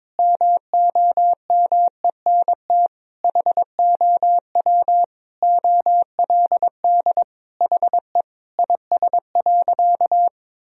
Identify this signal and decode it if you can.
Morse code, heard as a keyed tone, about 22 wpm, 700 Hz. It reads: MOMENT HOW OLD HE IS.
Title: Sentences from Top 1000 Words - Review Part 4 - 22wpm